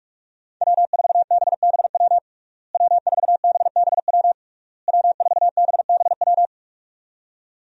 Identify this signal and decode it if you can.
Morse code, heard as a keyed tone, about 41 wpm, 700 Hz. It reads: W4BBW W4BBW W4BBW
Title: USA Call Signs - 5 Characters - Repeated 3x - 45wpm